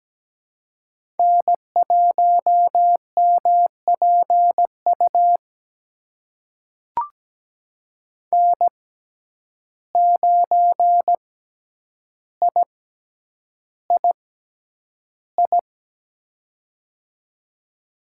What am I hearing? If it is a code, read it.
Morse code, heard as a keyed tone, about 17 wpm, spaced as for 9 wpm, 700 Hz. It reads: N1MPU N9III
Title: USA Call Signs - 5 Characters - 6x Character Spacing - 17wpm